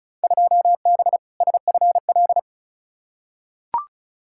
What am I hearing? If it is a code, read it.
Morse code, heard as a keyed tone, about 36 wpm, 700 Hz. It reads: 2B SFL
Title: Field Day Exchanges Encore - 35wpm